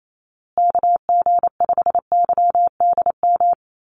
Morse code, heard as a keyed tone, about 28 wpm, 700 Hz. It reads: KG5YDM
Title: USA Call Signs Repeated 3x 28wpm